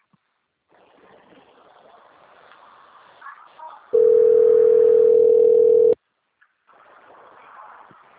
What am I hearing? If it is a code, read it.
Ringback tone (North America), 440 + 480 Hz.
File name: rg-1600-unknown- - - .35658